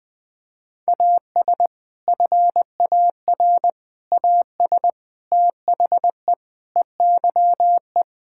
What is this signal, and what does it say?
Morse code, heard as a keyed tone, about 20 wpm, 700 Hz. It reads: AS FAR AS THE EYE
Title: Sentences with 4-Word Phrases - 20wpm